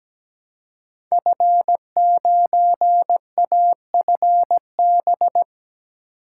Morse code, heard as a keyed tone, about 17 wpm, 700 Hz. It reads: F9AFB
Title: Code Groups - 5 Letter:Numbers - 17wpm